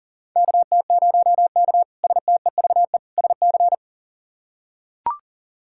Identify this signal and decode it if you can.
Morse code, heard as a keyed tone, about 40 wpm, 700 Hz. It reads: KT0K STEVE SC
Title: K1USN SST QSO Practice - 40wpm